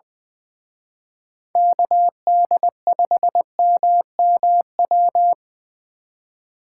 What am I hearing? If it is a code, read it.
Morse code, heard as a keyed tone, about 24 wpm, 700 Hz. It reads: KD5MMW
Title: Speed Racing - USA 2x3 Call Signs 20wpm